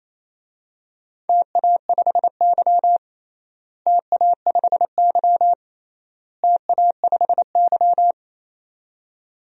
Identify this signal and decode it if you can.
Morse code, heard as a keyed tone, about 28 wpm, 700 Hz. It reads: TA5Y TA5Y TA5Y
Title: DXCC-100 Call Signs - 4 Characters - Repeated 3x - 28wpm